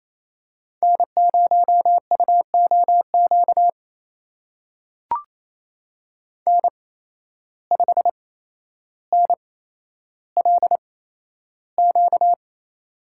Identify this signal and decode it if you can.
Morse code, heard as a keyed tone, about 28 wpm, spaced as for 12 wpm, 700 Hz. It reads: N0UOQ N5NLQ
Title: USA Call Signs - 8x Character Spacing - 28wpm